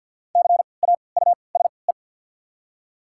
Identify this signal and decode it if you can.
Morse code, heard as a keyed tone, about 54 wpm, spaced as for 29 wpm, 700 Hz. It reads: CAUSE
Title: 3x Character Spacing - Top 300 Words - 50wpm